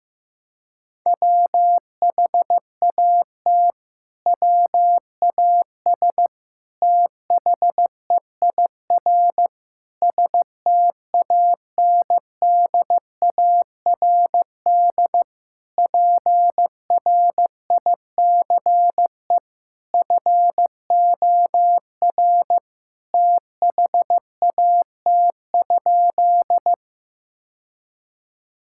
Morse code, heard as a keyed tone, about 15 wpm, 700 Hz. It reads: WHAT WAS THEIR STANDARD PRICE FOR THAT?
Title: Sentences from Top 1000 Words - Review Part 6 - 15wpm